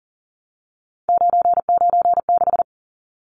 Morse code, heard as a keyed tone, about 40 wpm, 700 Hz. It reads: 996
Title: Numbers - 3-Digit 40wpm